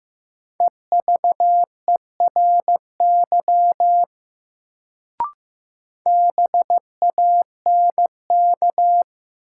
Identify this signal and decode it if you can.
Morse code, heard as a keyed tone, about 15 wpm, 700 Hz. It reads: EVERY BANK
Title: Sets of 5 Words - New Word Drill - 15wpm